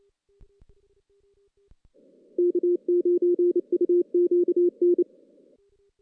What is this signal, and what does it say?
Morse code, heard as a keyed tone, about 29 wpm, 350 Hz. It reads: K9UQN
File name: 2016 JAN CW NAQP